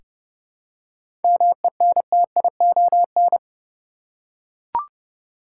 Morse code, heard as a keyed tone, about 30 wpm, 700 Hz. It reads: MENTION
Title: Seven Letter Words - 30wpm